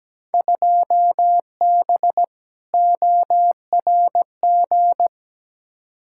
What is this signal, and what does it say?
Morse code, heard as a keyed tone, about 17 wpm, 700 Hz. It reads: 2B ORG
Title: Field Day Exchanges Encore - 17wpm